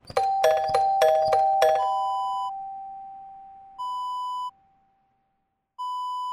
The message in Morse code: none